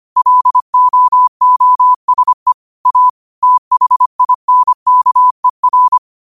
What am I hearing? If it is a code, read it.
Morse code, heard as a keyed tone, about 25 wpm, 1000 Hz. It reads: LOOSE A THINKER
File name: qotdfiles / 25wpm